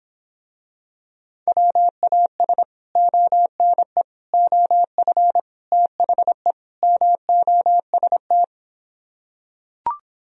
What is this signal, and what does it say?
Morse code, heard as a keyed tone, about 26 wpm, 700 Hz. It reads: WAS ONE OF THE MOST